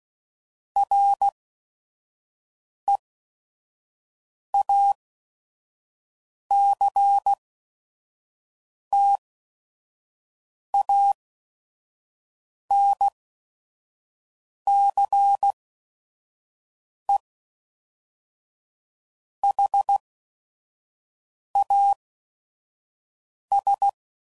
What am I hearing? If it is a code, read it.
Morse code, heard as a keyed tone, about 16 wpm, spaced as for 5 wpm, 800 Hz. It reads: REACTANCE HAS
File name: CW Code Groups @5WPM / Technical Subjects / XC